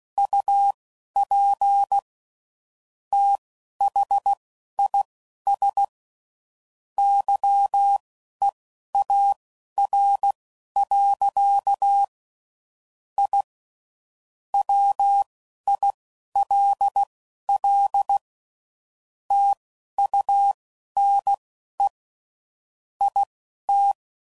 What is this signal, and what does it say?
Morse code, heard as a keyed tone, about 16 wpm, spaced as for 11 wpm, 800 Hz. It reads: UP THIS YEAR. I WILL TUNE IT
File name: CW Code Groups @10WPM / QSOs / QSO WD0FBT